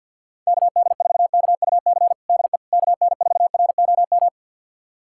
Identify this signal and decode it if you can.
Morse code, heard as a keyed tone, about 51 wpm, 700 Hz. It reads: KD4KUC DE KN4ROM
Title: USA Callsign DE Callsign 50wpm